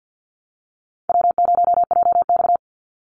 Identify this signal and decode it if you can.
Morse code, heard as a keyed tone, about 50 wpm, 700 Hz. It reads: W0JX
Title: Easy Call Signs - 50wpm